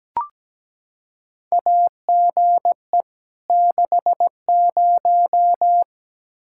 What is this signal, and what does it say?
Morse code, heard as a keyed tone, about 17 wpm, 700 Hz. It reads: AGE 60
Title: Practice QSO - Exchange 3 - More Info - 17wpm